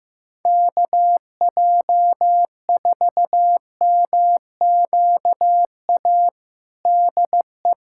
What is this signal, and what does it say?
Morse code, heard as a keyed tone, about 15 wpm, 700 Hz. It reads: KJ4MQA DE
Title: USA Callsign DE Callsign - 6 Characters - 15wpm